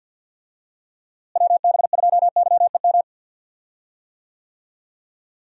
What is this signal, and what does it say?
Morse code, heard as a keyed tone, about 48 wpm, 700 Hz. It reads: WB2YEM
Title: USA Call Signs - 6 Characters - Repeated 3x - 50wpm